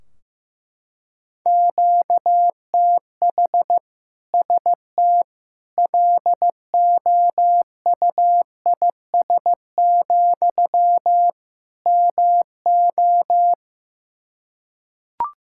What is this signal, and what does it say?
Morse code, heard as a keyed tone, about 15 wpm, 700 Hz. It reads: QTH ST LOUIS, MO
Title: Practice QSO - Exchange 1 - Beginning - 15wpm